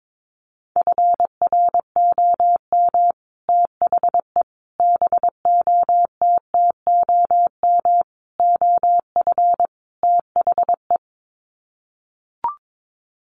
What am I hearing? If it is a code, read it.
Morse code, heard as a keyed tone, about 22 wpm, 700 Hz. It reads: FROM THE BOTTOM OF THE
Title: Sets of 5 Words Encore2 22wpm